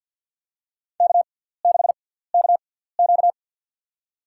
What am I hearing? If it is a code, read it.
Morse code, heard as a keyed tone, about 52 wpm, spaced as for 17 wpm, 700 Hz. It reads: K6KQ
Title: USA Call Signs - 4 Characters - 6x Character Spacing - 50wpm